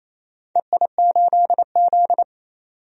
Morse code, heard as a keyed tone, about 28 wpm, 700 Hz. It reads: EI8Z
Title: DXCC-100 Call Signs - 4 Characters - 28wpm